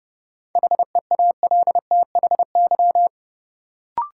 Morse code, heard as a keyed tone, about 31 wpm, 700 Hz. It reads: HEALTHY
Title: Sentences from Top 1000 Words - Review Part 5 - New Word Drill - 30wpm